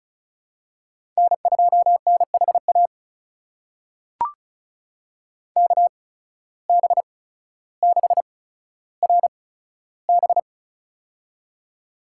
Morse code, heard as a keyed tone, about 36 wpm, spaced as for 15 wpm, 700 Hz. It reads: N2NHA KB6RB
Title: USA Call Signs - 5 Characters - 8x Character Spacing - 35wpm